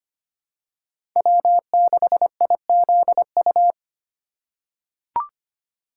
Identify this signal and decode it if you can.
Morse code, heard as a keyed tone, about 25 wpm, 700 Hz. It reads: W6IZU
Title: USA Call Signs - 5 Characters - 2x Character Spacing - 25wpm